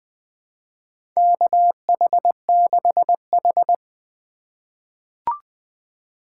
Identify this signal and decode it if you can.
Morse code, heard as a keyed tone, about 21 wpm, 700 Hz. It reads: KH6H
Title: USA Call Signs - 4 Characters - 3x Character Spacing - 20wpm